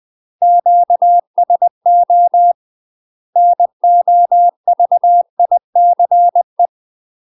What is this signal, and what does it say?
Morse code, heard as a keyed tone, about 20 wpm, 700 Hz. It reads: QSO NOVICE
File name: MASTER QSOs TRAINING/20WPM